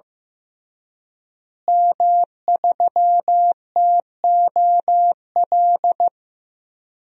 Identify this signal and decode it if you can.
Morse code, heard as a keyed tone, about 15 wpm, 700 Hz. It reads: M3TOL
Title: DXCC-100 Call Signs - 5 Characters - Repeated 3x - 15wpm